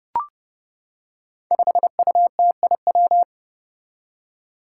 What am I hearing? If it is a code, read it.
Morse code, heard as a keyed tone, about 30 wpm, 700 Hz. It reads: 5UTIW
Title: Five Challenging Letter-Number Combinations 30wpm